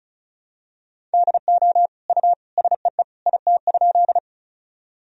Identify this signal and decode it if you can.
Morse code, heard as a keyed tone, about 36 wpm, 700 Hz. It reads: DO U SEE IT?
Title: Sentences from Top 100 Words and Abbreviations 35wpm